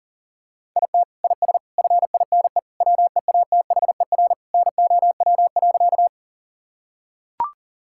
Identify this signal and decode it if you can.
Morse code, heard as a keyed tone, about 40 wpm, 700 Hz. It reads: IT IS FINE WEATHER NOW.
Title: Sentences from Top 1000 Words - Review Part 5 - 40wpm